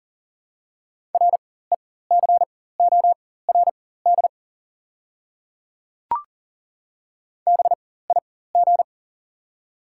Morse code, heard as a keyed tone, about 40 wpm, spaced as for 19 wpm, 700 Hz. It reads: RECORD BIG
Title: 4x Character Spacing - Top 500 Words - 40wpm